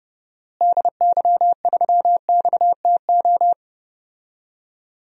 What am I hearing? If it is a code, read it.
Morse code, heard as a keyed tone, about 30 wpm, 700 Hz. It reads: DY3XTO
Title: DXCC-100 Call Signs - 6 Characters - Repeated 3x - 30wpm